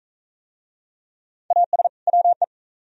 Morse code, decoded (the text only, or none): AS WE